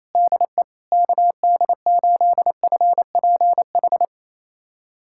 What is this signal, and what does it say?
Morse code, heard as a keyed tone, about 28 wpm, 700 Hz. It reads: DE KD8FPH
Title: USA Callsign DE Callsign 28wpm